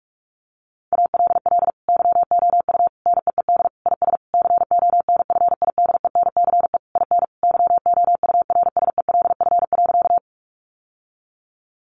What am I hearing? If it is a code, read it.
Morse code, heard as a keyed tone, about 46 wpm, 700 Hz. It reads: ALL YOU NEED IS CONFIDENCE IN YOURSELF.
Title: Sentences from Top 1500 Words 45wpm